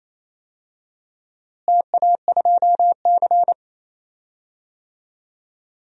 Morse code, heard as a keyed tone, about 28 wpm, 700 Hz. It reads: TA2C
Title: DXCC-100 Call Signs - 4 Characters - Repeated 3x - 28wpm